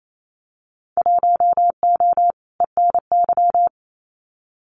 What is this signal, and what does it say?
Morse code, heard as a keyed tone, about 28 wpm, 700 Hz. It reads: 1O ENY
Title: Winter Field Day Exchanges 28wpm